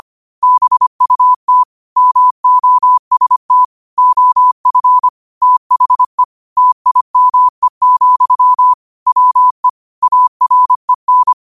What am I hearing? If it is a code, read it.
Morse code, heard as a keyed tone, about 25 wpm, 1000 Hz. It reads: BUT MOST OF THE TIME, WE AREN